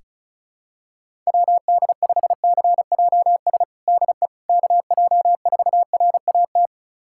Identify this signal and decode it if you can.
Morse code, heard as a keyed tone, about 34 wpm, 700 Hz. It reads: WD5CJS DE KJ4RAT
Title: USA Callsign DE Callsign 35wpm